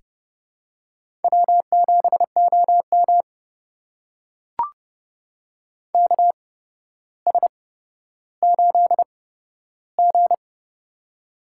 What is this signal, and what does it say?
Morse code, heard as a keyed tone, about 30 wpm, spaced as for 13 wpm, 700 Hz. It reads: W7OM KS8G
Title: USA Call Signs - 4 Characters - 8x Character Spacing - 30wpm